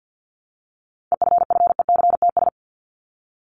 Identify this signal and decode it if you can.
Morse code, heard as a keyed tone, about 50 wpm, 700 Hz. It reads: EFFECTS
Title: ES:S Suffix - Top 100 Words with Base Words 50wpm